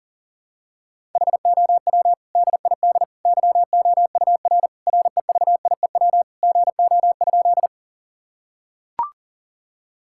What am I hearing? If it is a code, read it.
Morse code, heard as a keyed tone, about 40 wpm, 700 Hz. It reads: HOW DID YOUR REVIEW GO?
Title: Sentences from Top 1000 Words - Review Part 6 - 40wpm